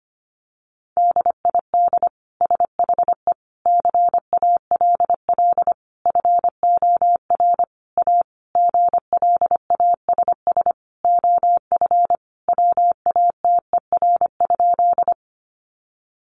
Morse code, heard as a keyed tone, about 25 wpm, 700 Hz. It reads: DID SHE CALL FOR A GLASS OF WATER?
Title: Sentences from Top 1300 Words 25wpm